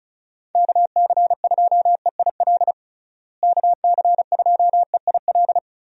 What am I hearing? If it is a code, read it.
Morse code, heard as a keyed tone, about 36 wpm, 700 Hz. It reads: KC2EIL KC2EIL
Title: USA Call Signs - 6 Characters - Repeated 2x - 35wpm